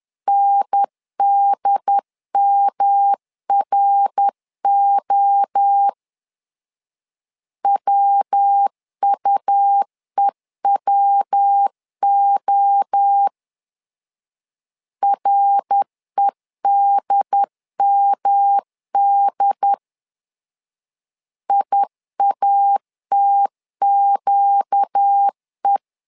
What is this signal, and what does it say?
Morse code, heard as a keyed tone, about 10 wpm, 800 Hz. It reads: NDMRO WUEWO REDMD IATQE
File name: LEZIONI / 4 Q 10wpm 2210